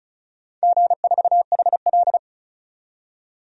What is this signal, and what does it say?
Morse code, heard as a keyed tone, about 36 wpm, 700 Hz. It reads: G4HL
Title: DXCC-100 Call Signs - 4 Characters - 35wpm